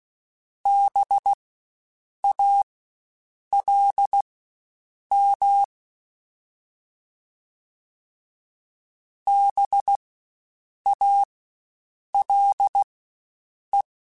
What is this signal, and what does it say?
Morse code, heard as a keyed tone, about 16 wpm, spaced as for 7 wpm, 800 Hz. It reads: BALM BALE
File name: Spaced Alphas / Spaced Alphas mp3s / Bs sp @15